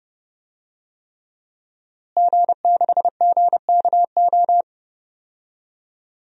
Morse code, heard as a keyed tone, about 30 wpm, 700 Hz. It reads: G6GKO
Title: DXCC-100 Call Signs - 5 Characters - Repeated 3x - 30wpm